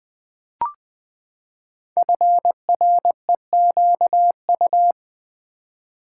Morse code, heard as a keyed tone, about 20 wpm, 700 Hz. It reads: FREQU
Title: Build A Word - 5 Letters At A Time - 10-Letter Words - 20wpm